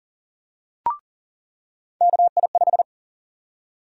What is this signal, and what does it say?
Morse code, heard as a keyed tone, about 40 wpm, 700 Hz. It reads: KI5
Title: USA Call Sign Prefixes - 40wpm